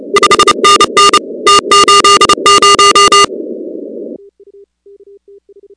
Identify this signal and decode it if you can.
Morse code, heard as a keyed tone, about 29 wpm, 1210 Hz, with noise. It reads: 5NN T80
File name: contests / 2019-cw-wpx-kt5j-k5pi- -2330 0-16